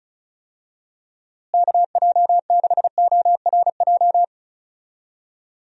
Morse code, heard as a keyed tone, about 36 wpm, 700 Hz. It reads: KJ6ORJ
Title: USA Call Signs - 6 Characters - Repeated 2x - 35wpm